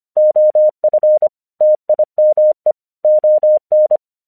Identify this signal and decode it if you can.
Morse code, heard as a keyed tone, about 25 wpm, 610 Hz. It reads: OF TIME ON
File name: 1 English 25wpm / AQC08